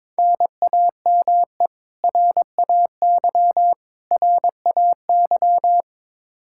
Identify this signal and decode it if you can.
Morse code, heard as a keyed tone, about 22 wpm, 700 Hz. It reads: NAME RAY RAY
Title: Practice QSO - Exchange 1 - Beginning - 22wpm